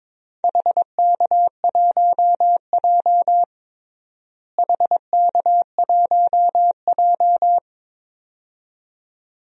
Morse code, heard as a keyed tone, about 22 wpm, 700 Hz. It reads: HK1J HK1J
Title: DXCC-100 Call Signs - 4 Characters - Repeated 2x - 22wpm